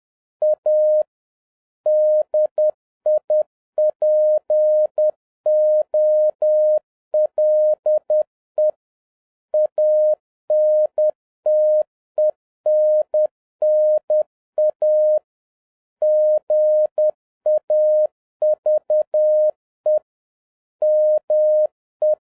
Morse code, heard as a keyed tone, about 10 wpm, 610 Hz. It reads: A DIPOLE ANTENNA GAVE ME